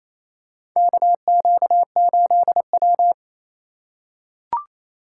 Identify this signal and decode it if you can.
Morse code, heard as a keyed tone, about 28 wpm, 700 Hz. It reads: KQ8W
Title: USA Call Signs - 4 Characters - 4x Character Spacing - 28wpm